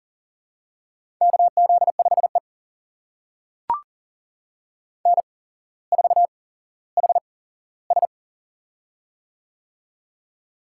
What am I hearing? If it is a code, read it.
Morse code, heard as a keyed tone, about 40 wpm, spaced as for 17 wpm, 700 Hz. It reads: KZ5E N4HS
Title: USA Call Signs - 4 Characters - 8x Character Spacing - 40wpm